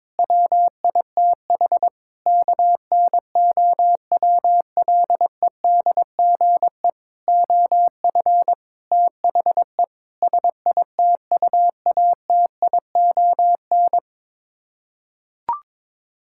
Morse code, heard as a keyed tone, about 22 wpm, 700 Hz. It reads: WITH KNOWLEDGE OF THE SITUATION
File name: Sets of 9 Words - 22wpm